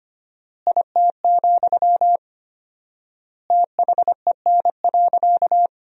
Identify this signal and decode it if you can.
Morse code, heard as a keyed tone, about 25 wpm, 700 Hz. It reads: IT, THEN.
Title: 4x Word-Spacing - Sentences from Top 200 Words - 25wpm